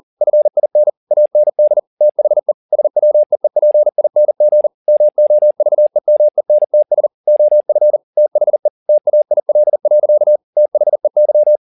Morse code, heard as a keyed tone, about 40 wpm, 600 Hz. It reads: FIN AND THE SWEEPING MOVEMENTS OF THE TAIL. THEY
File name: the old man and the sea 2 40wpm